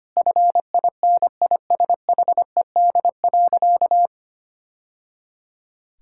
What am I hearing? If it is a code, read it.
Morse code, heard as a keyed tone, about 25 wpm, 700 Hz. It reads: FINISHED.
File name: practice / 25wpm / Sentences with 2-Word Phrases - 25wpm